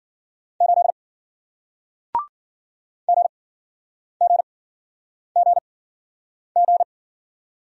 Morse code, heard as a keyed tone, about 53 wpm, spaced as for 9 wpm, 700 Hz. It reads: 8 GGGG